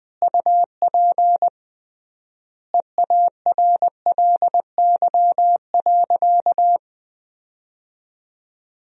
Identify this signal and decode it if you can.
Morse code, heard as a keyed tone, about 21 wpm, 700 Hz. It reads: UP EARLY.
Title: 3x Word-Spacing - Sentences from Top 400 Words - 20wpm